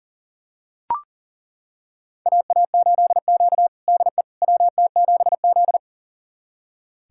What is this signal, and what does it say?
Morse code, heard as a keyed tone, about 40 wpm, 700 Hz. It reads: AA8Q DE WT7Z
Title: USA Callsign DE Callsign - 4 Characters - 40wpm